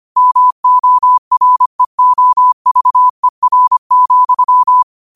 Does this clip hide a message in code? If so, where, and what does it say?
Yes: Morse code from 0.2 to 4.8 s, heard as a keyed tone, about 25 wpm, 1000 Hz. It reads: MOREOVER,